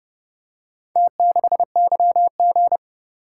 Morse code, heard as a keyed tone, about 30 wpm, 700 Hz. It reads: T6YG